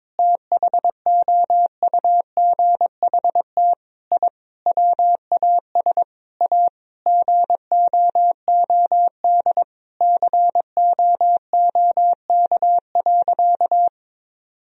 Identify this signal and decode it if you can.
Morse code, heard as a keyed tone, about 22 wpm, 700 Hz. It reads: THOUGHT I WAS A GOOD COOK.
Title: Sentences from Top 1000 Words - Review Part 4 - 22wpm